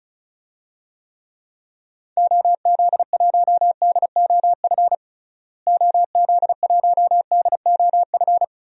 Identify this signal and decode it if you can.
Morse code, heard as a keyed tone, about 36 wpm, 700 Hz. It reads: OZ1DOF OZ1DOF
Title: DXCC-100 Call Signs - 6 Characters - Repeated 2x - 35wpm